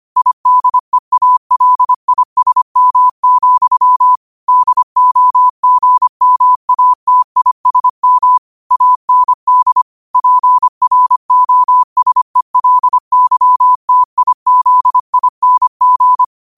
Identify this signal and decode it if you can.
Morse code, heard as a keyed tone, about 25 wpm, 1000 Hz. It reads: IDEALISM, DOGMATISM AND PROSELYTIZING
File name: qotdfiles / 25wpm